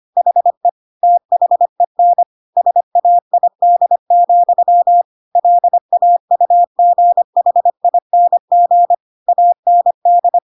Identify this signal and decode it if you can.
Morse code, heard as a keyed tone, about 25 wpm, 700 Hz. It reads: HE THEN SAID, LAUGHING AND
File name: Pinocchio / Pinocchio ch1 25 25 ik1hge